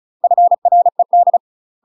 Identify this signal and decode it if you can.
Morse code, heard as a keyed tone, about 34 wpm, 700 Hz. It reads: FRED
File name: UK MASTER QSOs TRAINING / 35WPM / UK MASTER QSOs 0043